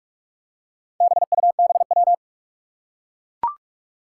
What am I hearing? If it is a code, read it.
Morse code, heard as a keyed tone, about 44 wpm, 700 Hz. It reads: BUBW
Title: Four Challenging Letter-Number Combinations 45wpm